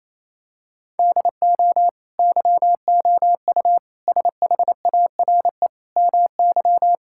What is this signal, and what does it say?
Morse code, heard as a keyed tone, about 28 wpm, 700 Hz. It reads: DO YOU SHARE MY
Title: Sentences from Top 1600 Words 28wpm